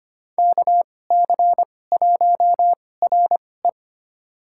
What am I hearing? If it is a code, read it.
Morse code, heard as a keyed tone, about 25 wpm, spaced as for 18 wpm, 700 Hz. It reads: KC1RE
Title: USA Call Signs - 2x Character Spacing - 25wpm